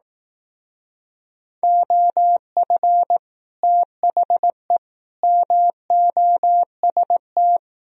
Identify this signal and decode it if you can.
Morse code, heard as a keyed tone, about 19 wpm, 700 Hz. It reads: OF THE MOST
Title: Speed-Racing - Sets of 3 Words 15wpm